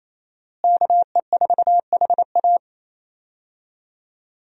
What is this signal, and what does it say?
Morse code, heard as a keyed tone, about 28 wpm, 700 Hz. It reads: KE4HA
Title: USA Call Signs - 5 Characters - Repeated 3x - 28wpm